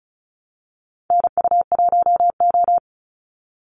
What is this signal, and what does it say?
Morse code, heard as a keyed tone, about 34 wpm, 700 Hz. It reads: NU1O
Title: Call Signs Repeated 2x - 35wpm